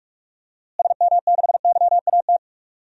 Morse code, heard as a keyed tone, about 46 wpm, 700 Hz. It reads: SM6YAT
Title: DXCC-100 Call Signs - 6 Characters - Repeated 2x - 45wpm